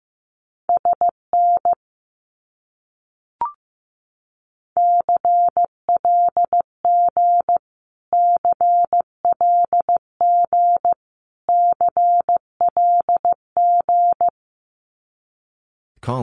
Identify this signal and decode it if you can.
Morse code, heard as a keyed tone, about 15 wpm, 700 Hz. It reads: SN CLG CLG CLG
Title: Abbreviations Repeated 3x - 15wpm